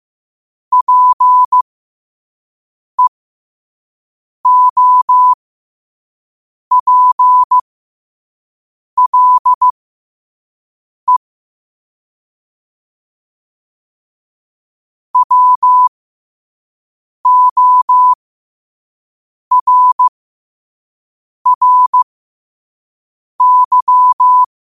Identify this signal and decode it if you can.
Morse code, heard as a keyed tone, about 15 wpm, spaced as for 5 wpm, 1000 Hz. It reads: PEOPLE WORRY
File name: qotdfiles / 5wpm